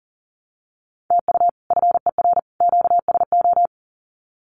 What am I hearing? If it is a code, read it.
Morse code, heard as a keyed tone, about 40 wpm, 700 Hz. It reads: TU FER QSO
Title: Common QSO Phrases 40wpm